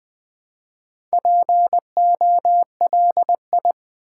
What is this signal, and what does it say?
Morse code, heard as a keyed tone, about 20 wpm, 700 Hz. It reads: POLI
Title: Build A Word - 2 Letters At A Time - 8-Letter Words - 20wpm